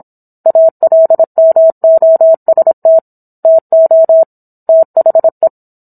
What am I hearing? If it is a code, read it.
Morse code, heard as a keyed tone, about 25 wpm, 650 Hz. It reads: ALMOST TO THE